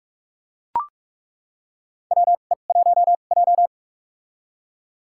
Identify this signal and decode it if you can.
Morse code, heard as a keyed tone, about 46 wpm, spaced as for 33 wpm, 690 Hz. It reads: WE1J
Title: USA Call Signs - 4 Characters - 2x Character Spacing - 45wpm